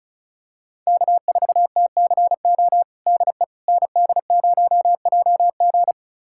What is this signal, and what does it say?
Morse code, heard as a keyed tone, about 34 wpm, 690 Hz. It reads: K4TCO DE ND0JG
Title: USA Callsign DE Callsign - 5 Characters - 35wpm